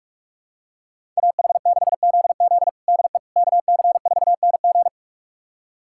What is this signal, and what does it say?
Morse code, heard as a keyed tone, about 41 wpm, 700 Hz. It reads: AH6ZZ DE KC4NG
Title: USA Callsign DE Callsign - 5 Characters - 45wpm